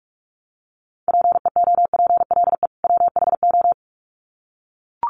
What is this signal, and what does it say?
Morse code, heard as a keyed tone, about 44 wpm, 700 Hz. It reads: PEOPLE WHO
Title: Sets of 2 Words 45wpm